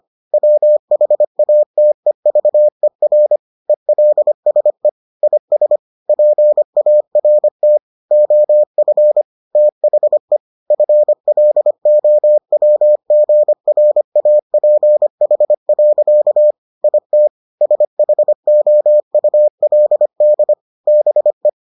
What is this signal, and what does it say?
Morse code, heard as a keyed tone, about 25 wpm, 600 Hz. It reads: WHATEVER ELSE IS PART OF THE FLOWGRAPH. IT SHOULD BE